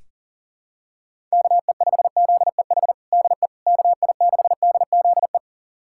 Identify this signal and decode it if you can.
Morse code, heard as a keyed tone, about 40 wpm, 700 Hz. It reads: KE5ZEH DE KI6DZE
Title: USA Callsign DE Callsign 40wpm